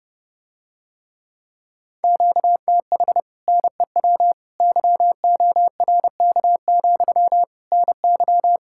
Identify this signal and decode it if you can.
Morse code, heard as a keyed tone, about 30 wpm, 700 Hz. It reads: QTH NEW YORK, NY